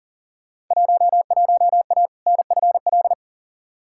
Morse code, heard as a keyed tone, about 40 wpm, 700 Hz. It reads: 11A NFL
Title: Field Day Exchanges Encore - 40wpm